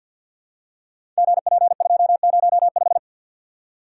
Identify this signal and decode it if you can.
Morse code, heard as a keyed tone, about 48 wpm, 700 Hz. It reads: GP205
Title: Code Groups - 5 Letter:Numbers - 50wpm